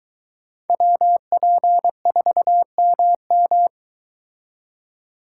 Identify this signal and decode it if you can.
Morse code, heard as a keyed tone, about 22 wpm, 700 Hz. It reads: WP4MM